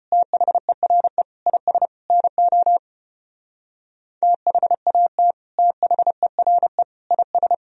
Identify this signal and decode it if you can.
Morse code, heard as a keyed tone, about 31 wpm, 700 Hz. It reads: THERE IS NO THAT THERE IS
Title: Speed-Racing - Sets of 4 Words 25wpm